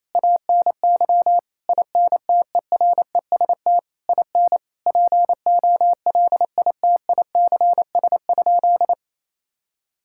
Morse code, heard as a keyed tone, about 27 wpm, 700 Hz. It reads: ANY INTEREST IN POLITICS?
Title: Sentences from Top 1000 Words - Review Part 6 - 28wpm